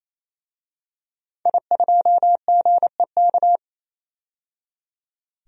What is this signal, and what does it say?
Morse code, heard as a keyed tone, about 28 wpm, 700 Hz. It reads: I2GEK
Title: DXCC-100 Call Signs - 5 Characters - 28wpm